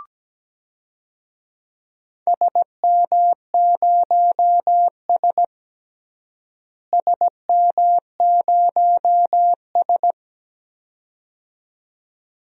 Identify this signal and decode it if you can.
Morse code, heard as a keyed tone, about 17 wpm, 700 Hz. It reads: SM0S SM0S
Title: DXCC-100 Call Signs - 4 Characters - Repeated 2x - 17wpm